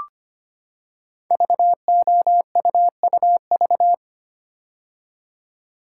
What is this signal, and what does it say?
Morse code, heard as a keyed tone, about 25 wpm, 700 Hz. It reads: VOUUV
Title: Code Groups - 5 Letters - 25wpm